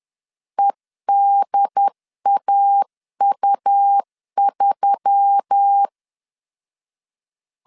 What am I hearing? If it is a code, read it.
Morse code, heard as a keyed tone, about 10 wpm, 800 Hz. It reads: EDAU3
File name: CW / DET / 1 6 1500C S C 10wpm 4252